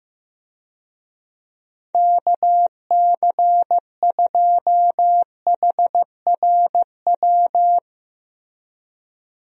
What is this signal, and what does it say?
Morse code, heard as a keyed tone, about 15 wpm, 700 Hz. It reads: KC2HRW
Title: USA Call Signs - 6 Characters - Repeated 3x - 15wpm